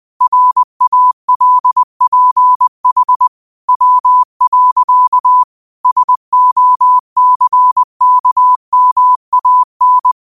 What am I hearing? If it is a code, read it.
Morse code, heard as a keyed tone, about 20 wpm, 1000 Hz. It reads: RALPH W. SOCKMAN